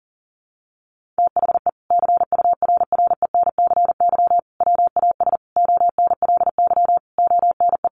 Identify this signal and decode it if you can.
Morse code, heard as a keyed tone, about 40 wpm, 700 Hz. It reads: THE CURRENCY WAS ONLY ONE
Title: Sentences from Top 1500 Words 40wpm